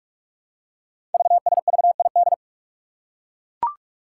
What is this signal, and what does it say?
Morse code, heard as a keyed tone, about 44 wpm, 700 Hz. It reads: VSVID
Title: Code Groups - 5 Letters - 45wpm